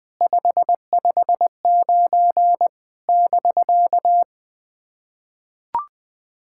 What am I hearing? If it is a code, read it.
Morse code, heard as a keyed tone, about 20 wpm, 700 Hz. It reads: 559 <BK>
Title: SOTA QSO Practice 20wpm